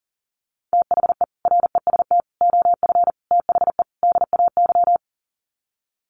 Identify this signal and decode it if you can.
Morse code, heard as a keyed tone, about 40 wpm, 700 Hz. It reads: THE REST OF THE DAY
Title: Sets of 5 Words 40wpm